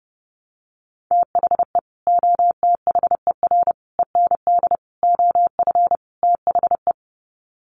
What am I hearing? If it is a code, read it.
Morse code, heard as a keyed tone, about 30 wpm, 700 Hz. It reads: THE OTHER END OF THE
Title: Sets of 5 Words 30wpm